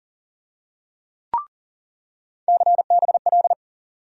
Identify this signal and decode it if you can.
Morse code, heard as a keyed tone, about 40 wpm, 700 Hz. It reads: CBL
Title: USA Call Sign Postfixes - 3 Letters 40wpm